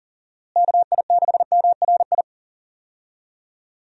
Morse code, heard as a keyed tone, about 40 wpm, 700 Hz. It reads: KI6MRI
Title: USA Call Signs - 6 Characters - Repeated 3x - 40wpm